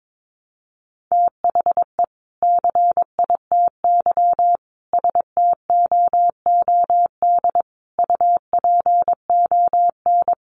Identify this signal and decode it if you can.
Morse code, heard as a keyed tone, about 22 wpm, 700 Hz. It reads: THE CITY STOOD UPON
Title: Sentences from Top 1100 Words 22wpm